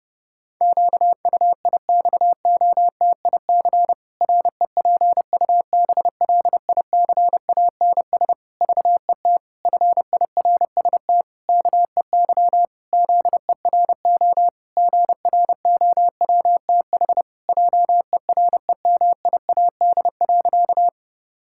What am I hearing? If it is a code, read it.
Morse code, heard as a keyed tone, about 32 wpm, 700 Hz. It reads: QUIXOTIC REPUBLICANS VET FIRST KEY ZERO GROWTH JEREMIAD.
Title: Pangram Challenge v2 - 30wpm